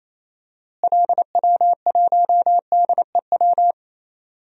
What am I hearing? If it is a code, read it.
Morse code, heard as a keyed tone, about 28 wpm, 700 Hz. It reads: LW1DEW
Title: DXCC-100 Call Signs - 6 Characters - Repeated 2x - 28wpm